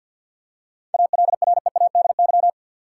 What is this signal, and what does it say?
Morse code, heard as a keyed tone, about 53 wpm, 700 Hz. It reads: ALREADY
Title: Seven Letter Words - 50wpm